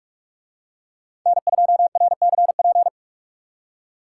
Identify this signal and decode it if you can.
Morse code, heard as a keyed tone, about 40 wpm, 700 Hz. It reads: N2RCP